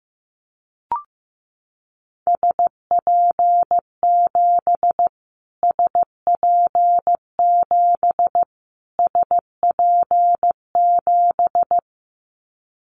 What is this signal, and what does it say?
Morse code, heard as a keyed tone, about 15 wpm, 700 Hz. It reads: SP7 SP7 SP7
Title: Callsign Prefixes for Top 25 DXCC - Repeated 3x - 15wpm